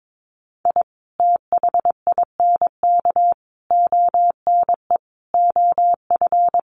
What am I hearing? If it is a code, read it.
Morse code, heard as a keyed tone, about 22 wpm, 700 Hz. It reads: I THINK ONE OF
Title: Sets of 5 Words 22wpm